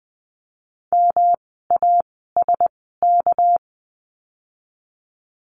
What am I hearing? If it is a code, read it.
Morse code, heard as a keyed tone, about 20 wpm, spaced as for 15 wpm, 700 Hz. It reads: MASK
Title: 199 Lesson 32f - Letter k - Words 20wpm